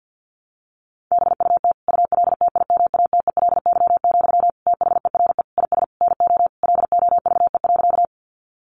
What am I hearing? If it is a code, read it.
Morse code, heard as a keyed tone, about 48 wpm, 700 Hz. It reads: BUT ULTIMATELY, THERE IS NO LOVE.
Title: Sentences from Top 1200 Words 50wpm